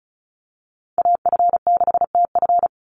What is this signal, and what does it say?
Morse code, heard as a keyed tone, about 34 wpm, 700 Hz. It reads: AF6TF
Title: USA Call Signs 35wpm